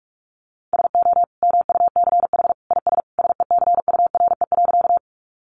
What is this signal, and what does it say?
Morse code, heard as a keyed tone, about 44 wpm, 700 Hz. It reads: SO MUCH IS SECURE.
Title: Sentences from Top 1200 Words 45wpm